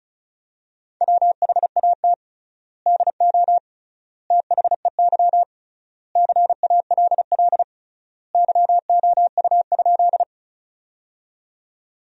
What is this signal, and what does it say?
Morse code, heard as a keyed tone, about 36 wpm, 700 Hz. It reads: WHAT DO THEY CALL YOU?
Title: 3x Word-Spacing - Sentences from Top 100 Words - 35wpm